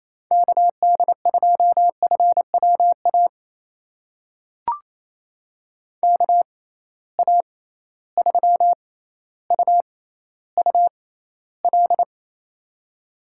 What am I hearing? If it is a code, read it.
Morse code, heard as a keyed tone, about 28 wpm, spaced as for 14 wpm, 700 Hz. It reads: KD2FWA KA3UUL
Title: USA Call Signs - 6 Characters - 6x Character Spacing - 28wpm